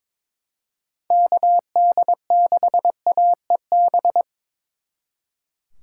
Morse code, heard as a keyed tone, about 25 wpm, 700 Hz. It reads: KD6AEB
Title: Speed Racing - USA 2x3 Call Signs 22wpm